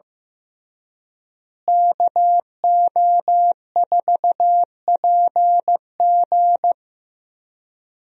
Morse code, heard as a keyed tone, about 15 wpm, 700 Hz. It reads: KO4PG
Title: USA Call Signs - 5 Characters - Repeated 3x - 15wpm